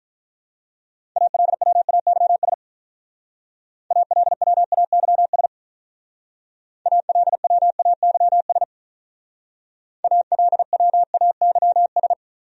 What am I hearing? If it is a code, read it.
Morse code, heard as a keyed tone, about 43 wpm, 700 Hz. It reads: ALWAYS ALWAYS ALWAYS ALWAYS
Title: Speed-Racing Top 300 Words 35wpm